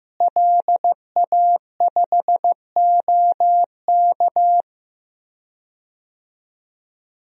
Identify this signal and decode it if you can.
Morse code, heard as a keyed tone, about 15 wpm, 700 Hz. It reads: LA5OK
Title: DXCC-100 Call Signs - 5 Characters - Repeated 3x - 15wpm